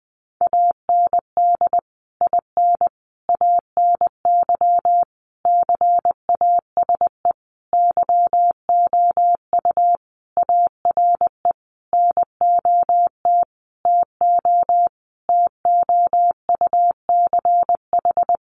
Morse code, heard as a keyed tone, about 20 wpm, 700 Hz. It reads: AND IN ANY CASE YOU ARE NOT TO TOUCH
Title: Sentences from Top 1200 Words 20wpm